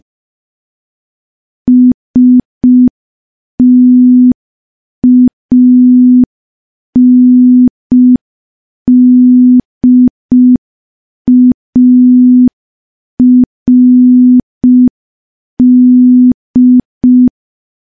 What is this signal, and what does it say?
Morse code, heard as a keyed tone, about 5 wpm, 250 Hz. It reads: STANDARD
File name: morse / morse-training-05WPM